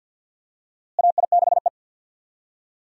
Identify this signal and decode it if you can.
Morse code, heard as a keyed tone, about 47 wpm, 700 Hz. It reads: AI6E
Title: USA Call Signs - 4 Characters - Repeated 3x - 50wpm